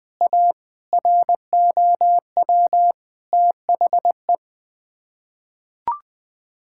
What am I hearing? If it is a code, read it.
Morse code, heard as a keyed tone, about 20 wpm, 700 Hz. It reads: A ROW THE
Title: Sets of 8 Words - 20wpm